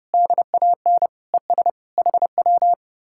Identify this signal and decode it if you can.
Morse code, heard as a keyed tone, about 40 wpm, spaced as for 35 wpm, 700 Hz. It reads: DAN ES HW
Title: Speed-Racing Common QSO Phrases - 30wpm